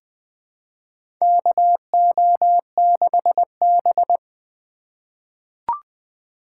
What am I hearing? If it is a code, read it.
Morse code, heard as a keyed tone, about 20 wpm, 700 Hz. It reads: KO6B
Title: USA Call Signs - 4 Characters - 4x Character Spacing - 20wpm